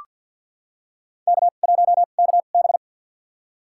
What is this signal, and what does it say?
Morse code, heard as a keyed tone, about 47 wpm, spaced as for 35 wpm, 700 Hz. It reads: K1KB
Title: USA Call Signs - 4 Characters - 2x Character Spacing - 50wpm